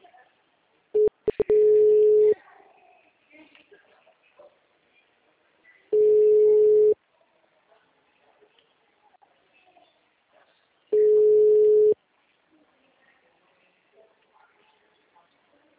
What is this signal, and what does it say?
Ringback tone (Europe, 425 Hz), with speech.
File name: out- -2015- - - .48817